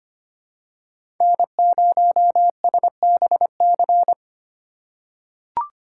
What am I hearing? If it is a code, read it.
Morse code, heard as a keyed tone, about 25 wpm, 700 Hz. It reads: N0SBC
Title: USA Call Signs - 5 Characters - 2x Character Spacing - 25wpm